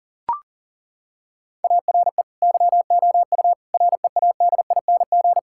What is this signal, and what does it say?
Morse code, heard as a keyed tone, about 40 wpm, 700 Hz. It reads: ARE YOU READING
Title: Sentences from Top 1000 Words - Review Part 6 - 40wpm